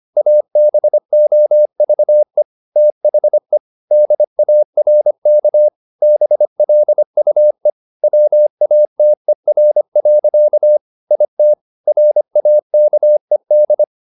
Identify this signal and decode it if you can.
Morse code, heard as a keyed tone, about 25 wpm, 600 Hz. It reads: ABOVE THE DARK BLUE WATER. IT RAKED